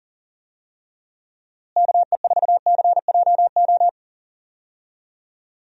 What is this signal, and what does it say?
Morse code, heard as a keyed tone, about 37 wpm, 700 Hz. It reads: KE4CJO
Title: USA Call Signs - 6 Characters - Repeated 2x - 40wpm